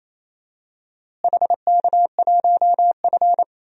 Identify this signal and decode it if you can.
Morse code, heard as a keyed tone, about 27 wpm, 700 Hz. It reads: HK1F
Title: DXCC-100 Call Signs - 4 Characters - Repeated 3x - 28wpm